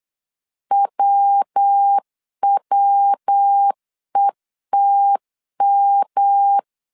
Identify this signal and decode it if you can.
Morse code, heard as a keyed tone, about 8 wpm, 800 Hz. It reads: WWETM